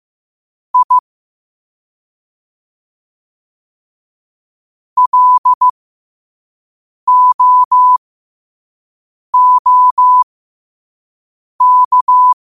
Morse code, heard as a keyed tone, about 15 wpm, spaced as for 5 wpm, 1000 Hz. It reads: I LOOK
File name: qotdfiles / 5wpm